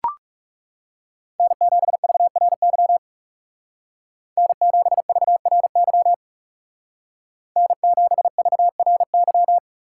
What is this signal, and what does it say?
Morse code, heard as a keyed tone, about 36 wpm, 700 Hz. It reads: N7VRY N7VRY N7VRY
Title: Speed Racing - USA 1x3 Call Signs 30wpm